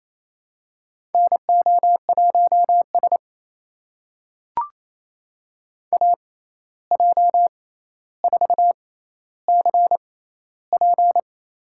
Morse code, heard as a keyed tone, about 28 wpm, spaced as for 13 wpm, 700 Hz. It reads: NO1S AJ4CP
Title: USA Call Signs - 6x Character Spacing - 28wpm